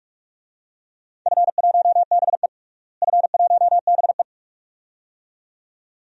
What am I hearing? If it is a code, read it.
Morse code, heard as a keyed tone, about 44 wpm, 700 Hz. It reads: F1BE F1BE
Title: DXCC-100 Call Signs - 4 Characters - Repeated 2x - 45wpm